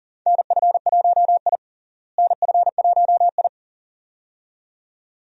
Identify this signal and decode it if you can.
Morse code, heard as a keyed tone, about 40 wpm, 700 Hz. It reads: NF1I NF1I
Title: USA Call Signs - 4 Characters - Repeated 2x - 40wpm